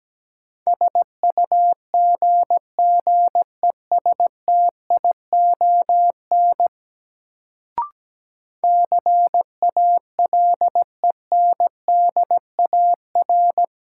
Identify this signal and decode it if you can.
Morse code, heard as a keyed tone, about 17 wpm, 700 Hz. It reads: SUGGESTION CALENDAR
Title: Sentences from Top 1600 Words - New Word Drill Alt - 17wpm